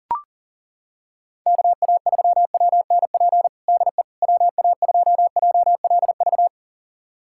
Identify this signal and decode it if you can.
Morse code, heard as a keyed tone, about 40 wpm, 700 Hz. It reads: KA3WNP DE WA2JLV
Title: USA Callsign DE Callsign - 6 Characters - 40wpm